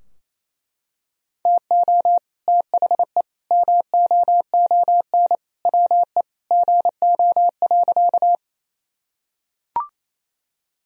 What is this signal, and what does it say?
Morse code, heard as a keyed tone, about 28 wpm, 700 Hz. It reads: TO THE MOON WE GO.
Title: Sentences from Top 1000 Words - Review Part 3 - 28wpm